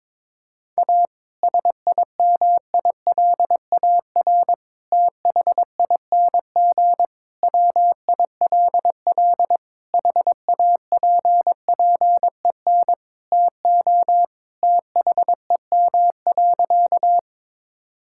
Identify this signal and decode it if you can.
Morse code, heard as a keyed tone, about 22 wpm, 700 Hz. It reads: A SIMILAR THING WILL HAPPEN TO THEM.
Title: Sentences from Top 1000 Words - Review Part 5 - 22wpm